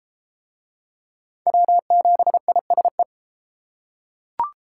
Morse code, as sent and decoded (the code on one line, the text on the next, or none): .-- --... .. ... .
W7ISE